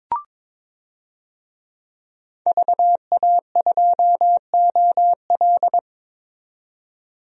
Morse code, heard as a keyed tone, about 22 wpm, 700 Hz. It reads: VA2OL
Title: DXCC-100 Call Signs - 5 Characters - 22wpm